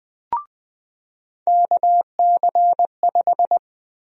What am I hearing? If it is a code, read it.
Morse code, heard as a keyed tone, about 20 wpm, 700 Hz. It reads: KC5
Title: USA Call Sign Prefixes - 20wpm